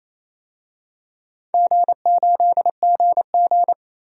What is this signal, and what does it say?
Morse code, heard as a keyed tone, about 28 wpm, 700 Hz. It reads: G8GG
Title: DXCC-100 Call Signs - 4 Characters - 28wpm